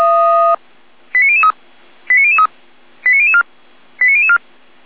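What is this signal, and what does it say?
Five-tone selective call (ZVEI), heard as quick runs of five tones, tones of about 70 ms. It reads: 89003 89003 89004 89004
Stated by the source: ZVEI five-tone call received by SDR.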